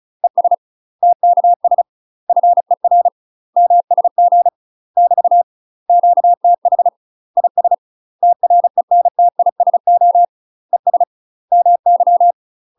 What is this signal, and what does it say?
Morse code, heard as a keyed tone, about 36 wpm, 700 Hz. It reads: ES TKS FER MSG <BT> QTH IS TRENTISO ES MY
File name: UK MASTER QSOs TRAINING / 35WPM / UK MASTER QSOs 0014